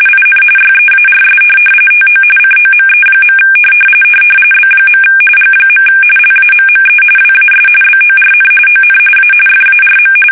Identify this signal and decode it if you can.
RTTY, about 45.45 baud. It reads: IN YEARS TO COME. B. G. LAMME WAS BORN ON A FARM
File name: fsk850 / benjamin-g-lamme